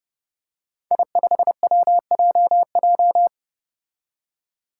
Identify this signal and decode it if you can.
Morse code, heard as a keyed tone, about 30 wpm, 700 Hz. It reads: I5WJJ
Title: Five Challenging Letter-Number Combinations 30wpm